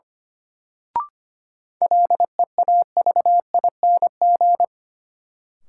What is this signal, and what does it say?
Morse code, heard as a keyed tone, about 25 wpm, 700 Hz. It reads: LEAVING
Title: Sentences from Top 1000 Words - Review Part 4 - New Word Drill - 25wpm